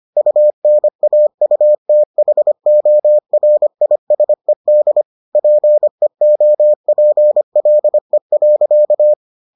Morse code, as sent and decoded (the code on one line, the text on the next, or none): ..- -. .- ..- - .... --- .-. .. ... -.. / .--. --- .--. .-.. .-.-.-
UNAUTHORISED PEOPLE.